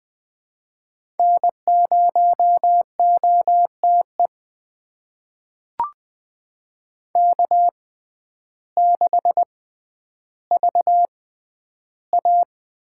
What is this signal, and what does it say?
Morse code, heard as a keyed tone, about 20 wpm, spaced as for 11 wpm, 700 Hz. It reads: N0OTE K6VA
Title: USA Call Signs - 5 Characters - 6x Character Spacing - 20wpm